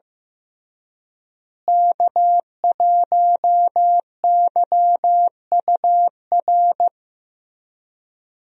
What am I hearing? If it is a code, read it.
Morse code, heard as a keyed tone, about 15 wpm, 700 Hz. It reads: K1YUR
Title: USA Call Signs - 5 Characters - Repeated 3x - 15wpm